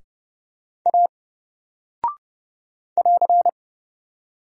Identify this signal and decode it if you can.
Morse code, heard as a keyed tone, about 30 wpm, 700 Hz. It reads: A <AR>
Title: Single Character (Letter:Number:Punctuation:Prosign) Alt 30wpm